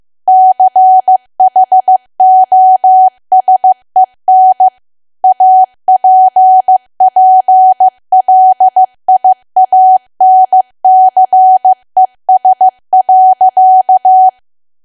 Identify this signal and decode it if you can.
Morse code, heard as a keyed tone, about 15 wpm, 750 Hz. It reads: CHOSEN APPLIANCES.
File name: cwa / W1AW-Level2 / 15WPM